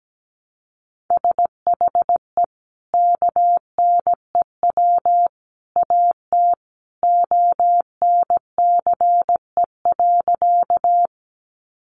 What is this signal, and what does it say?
Morse code, heard as a keyed tone, about 17 wpm, 700 Hz. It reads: SHE KNEW AT ONCE.
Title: Sentences from Top 400 Words 17wpm